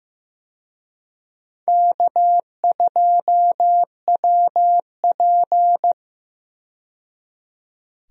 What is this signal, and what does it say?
Morse code, heard as a keyed tone, about 16 wpm, 700 Hz. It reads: K2WP
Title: USA Call Signs - 4 Characters - 15wpm